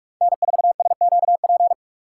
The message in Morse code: N4SQP

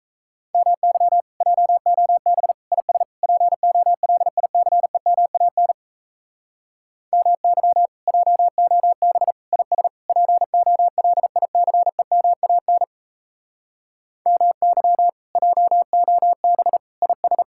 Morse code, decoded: MY JOB IS POLICEMAN MY JOB IS POLICEMAN MY JOB IS

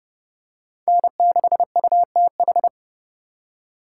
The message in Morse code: N6UTH

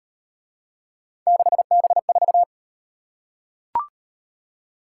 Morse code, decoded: 6B4

